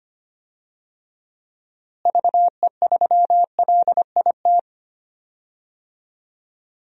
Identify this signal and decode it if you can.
Morse code, heard as a keyed tone, about 25 wpm, 700 Hz. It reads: VE3LIT
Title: DXCC-100 Call Signs - 6 Characters - Repeated 3x - 25wpm